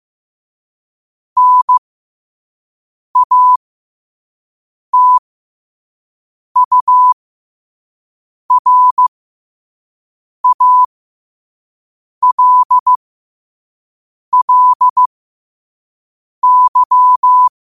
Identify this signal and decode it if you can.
Morse code, heard as a keyed tone, about 15 wpm, spaced as for 5 wpm, 1000 Hz. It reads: NATURALLY
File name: qotdfiles / 5wpm